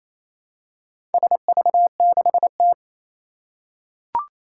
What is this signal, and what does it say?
Morse code, heard as a keyed tone, about 27 wpm, 700 Hz. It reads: SV6T
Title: Four Challenging Letter-Number Combinations 28wpm